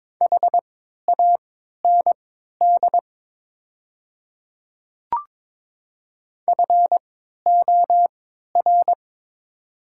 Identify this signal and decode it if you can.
Morse code, heard as a keyed tone, about 22 wpm, spaced as for 13 wpm, 690 Hz. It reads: HAND FOR
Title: 3x Character Spacing - Top 200 Words - 22wpm